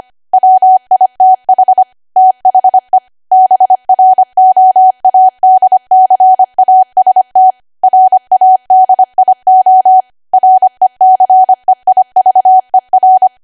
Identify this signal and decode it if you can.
Morse code, heard as a keyed tone, about 25 wpm, 750 Hz. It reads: WITH THE BROADCAST RADIO RECEIVER
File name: Archive / 25 WPM / 25WPM